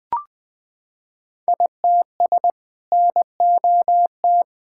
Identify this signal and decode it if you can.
Morse code, heard as a keyed tone, about 20 wpm, 700 Hz. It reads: ITS NOT
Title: Sentences with 6-Word Phrases - 20wpm